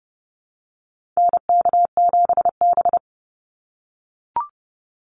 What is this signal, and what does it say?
Morse code, heard as a keyed tone, about 30 wpm, 700 Hz. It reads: NK7B
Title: Call Signs - 30wpm